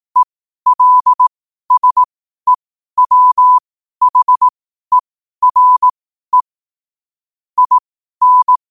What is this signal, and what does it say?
Morse code, heard as a keyed tone, about 18 wpm, spaced as for 12 wpm, 1000 Hz. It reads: ELSEWHERE IN
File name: qotdfiles / 12wpm